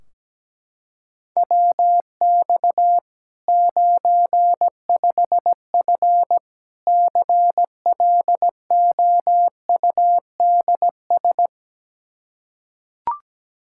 Morse code, heard as a keyed tone, about 17 wpm, 700 Hz. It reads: WX 95F CLOUDS
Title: K1USN SST QSO Practice 17wpm